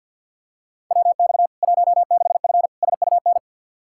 Morse code, heard as a keyed tone, about 48 wpm, 700 Hz. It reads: WX 16F SUN
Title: K1USN SST QSO Practice 50wpm